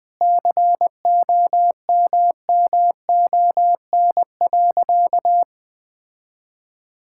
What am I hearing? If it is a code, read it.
Morse code, heard as a keyed tone, about 21 wpm, 700 Hz. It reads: COMMON.
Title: Sentences with 6-Word Phrases - 20wpm